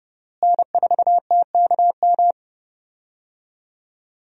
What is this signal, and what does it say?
Morse code, heard as a keyed tone, about 31 wpm, 700 Hz. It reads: N4TKM
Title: USA Call Signs - 5 Characters - Repeated 3x - 30wpm